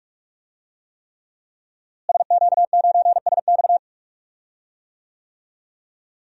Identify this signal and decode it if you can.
Morse code, heard as a keyed tone, about 46 wpm, 700 Hz. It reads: SQ9SX
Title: DXCC-100 Call Signs - 5 Characters - Repeated 3x - 45wpm